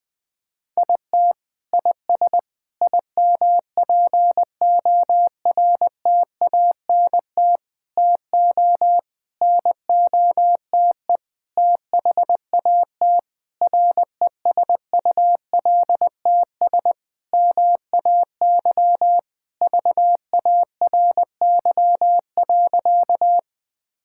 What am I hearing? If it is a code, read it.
Morse code, heard as a keyed tone, about 20 wpm, 700 Hz. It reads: IT IS IMPORTANT TO NOTE THAT RESULTS MAY VARY.
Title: Sentences with 6-Word Phrases - 20wpm